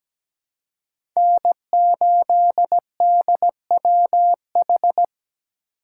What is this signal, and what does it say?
Morse code, heard as a keyed tone, about 17 wpm, 700 Hz. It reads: N8DWH
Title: USA Call Signs - 4x Character Spacing - 17wpm